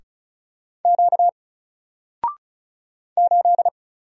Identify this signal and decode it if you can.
Morse code, heard as a keyed tone, about 35 wpm, 700 Hz. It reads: Q 8
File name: Single-Letter-Number-35wpm